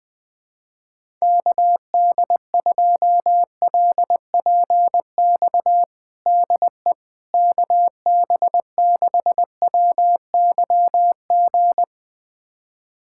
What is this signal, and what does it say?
Morse code, heard as a keyed tone, about 21 wpm, 700 Hz. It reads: KD2LPX DE KB6WYG
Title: USA Callsign DE Callsign - 6 Characters - 20wpm